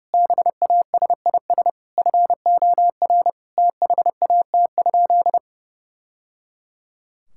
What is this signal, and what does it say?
Morse code, heard as a keyed tone, about 30 wpm, 700 Hz. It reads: BASIS FOR THAT?
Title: Sentences from Top 1000 Words - Review Part 5 - 30wpm